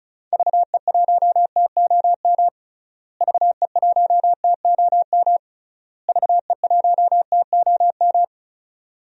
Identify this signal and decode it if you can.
Morse code, heard as a keyed tone, about 34 wpm, 700 Hz. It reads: VE1TOM VE1TOM VE1TOM
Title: DXCC-100 Call Signs - 6 Characters - Repeated 3x - 35wpm